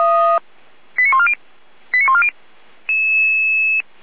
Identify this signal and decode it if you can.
Five-tone selective call (ZVEI), heard as quick runs of five tones, tones of about 70 ms. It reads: 89140 89140